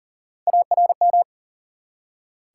Morse code, heard as a keyed tone, about 42 wpm, 700 Hz. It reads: ARM
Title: Three Letter Words - 40wpm